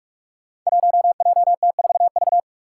Morse code, heard as a keyed tone, about 46 wpm, 700 Hz. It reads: 1JT4V
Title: Five Challenging Letter-Number Combinations 45wpm